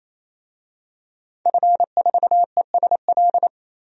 Morse code, heard as a keyed tone, about 27 wpm, 700 Hz. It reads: F4ESL
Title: DXCC-100 Call Signs - 5 Characters - 28wpm